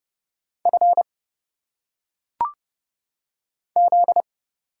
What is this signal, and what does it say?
Morse code, heard as a keyed tone, about 29 wpm, 700 Hz. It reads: F Z
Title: Single Letters - Phonetic Pronunciation 30wpm